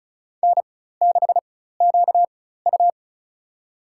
Morse code, heard as a keyed tone, about 36 wpm, spaced as for 16 wpm, 700 Hz. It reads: N6QU
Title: USA Call Signs - 4x Character Spacing - 35wpm